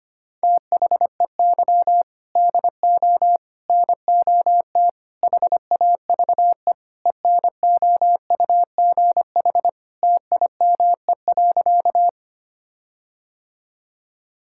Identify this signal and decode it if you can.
Morse code, heard as a keyed tone, about 25 wpm, 700 Hz. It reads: THEY DO NOT HAVE ENOUGH TIME.
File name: practice / 25wpm / Sentences with 4-Word Phrases - 25wpm